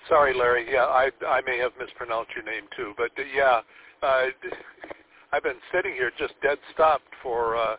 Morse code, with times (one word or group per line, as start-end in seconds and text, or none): none